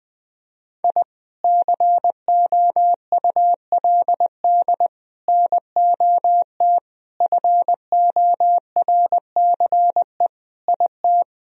Morse code, heard as a keyed tone, about 20 wpm, 700 Hz. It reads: I COULD NOT FORCE IT
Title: Sentences from Top 1000 Words - Review Part 3 - 20wpm